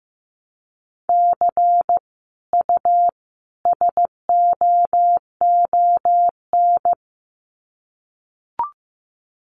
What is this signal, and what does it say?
Morse code, heard as a keyed tone, about 15 wpm, 700 Hz. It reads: C U SOON
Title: Common QSO Phrases 15wpm